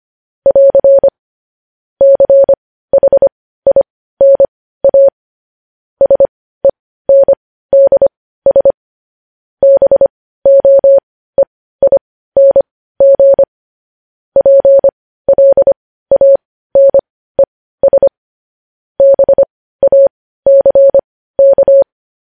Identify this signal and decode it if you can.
Morse code, heard as a keyed tone, about 25 wpm, spaced as for 15 wpm, 550 Hz. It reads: <AR> CHINA SENDS BOEING PLANES BACK
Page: news headlines in morse code at 15 wpm